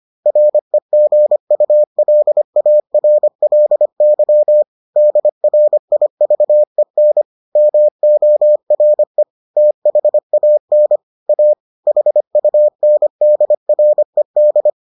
Morse code, heard as a keyed tone, about 25 wpm, 600 Hz. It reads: REGULARLY DRIVEN MORE THAN A HUNDRED